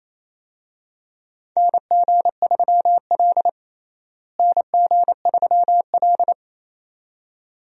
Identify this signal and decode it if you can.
Morse code, heard as a keyed tone, about 28 wpm, 700 Hz. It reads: NG3L NG3L
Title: USA Call Signs - 4 Characters - Repeated 2x - 28wpm